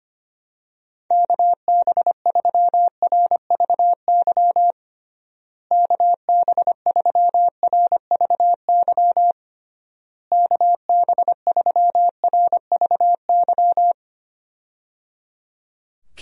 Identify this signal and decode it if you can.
Morse code, heard as a keyed tone, about 25 wpm, 700 Hz. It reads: KB3RVY KB3RVY KB3RVY
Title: USA Call Signs - 6 Characters - Repeated 3x - 25wpm